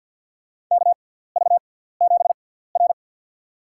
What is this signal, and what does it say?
Morse code, heard as a keyed tone, about 48 wpm, spaced as for 17 wpm, 700 Hz. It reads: KV7R
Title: USA Call Signs - 4 Characters - 6x Character Spacing - 50wpm